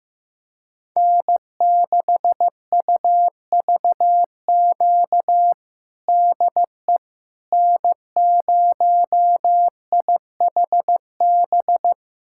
Morse code, heard as a keyed tone, about 15 wpm, 700 Hz. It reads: N6UVQ DE N0IHB
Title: USA Callsign DE Callsign - 5 Characters - 15wpm